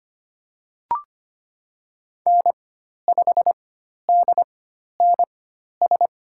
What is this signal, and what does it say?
Morse code, heard as a keyed tone, about 25 wpm, spaced as for 12 wpm, 700 Hz. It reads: N5DNS
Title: USA Call Signs - 5 Characters - 4x Character Spacing - 25wpm